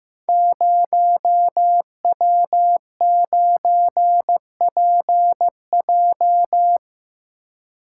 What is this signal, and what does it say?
Morse code, heard as a keyed tone, about 15 wpm, 700 Hz. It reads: 0W9PJ
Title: Code Groups - 5 Letter:Numbers - 15wpm